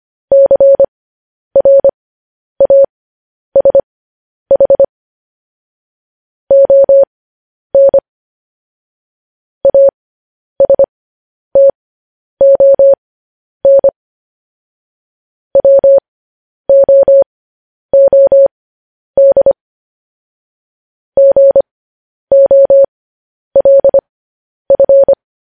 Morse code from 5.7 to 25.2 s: ON ASTON WOOD GOLF

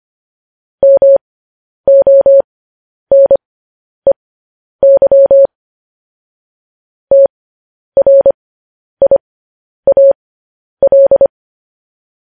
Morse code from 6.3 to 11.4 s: TRIAL

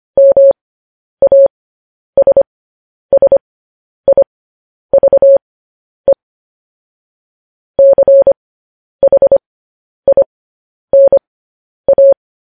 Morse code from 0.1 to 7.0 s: MASSIVE

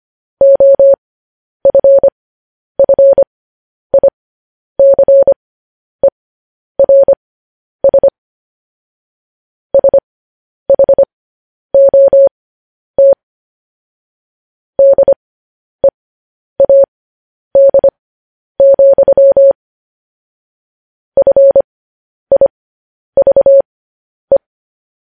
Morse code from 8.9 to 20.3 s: SHOT DEAD,

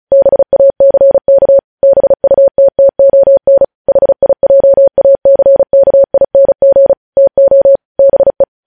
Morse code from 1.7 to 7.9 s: BUTTON HIJACKING TO

